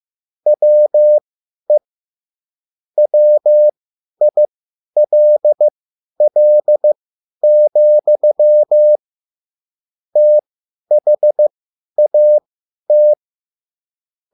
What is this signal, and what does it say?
Morse code, heard as a keyed tone, about 15 wpm, spaced as for 10 wpm, 600 Hz. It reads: WE WILL, THAT
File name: QSL / CW lesson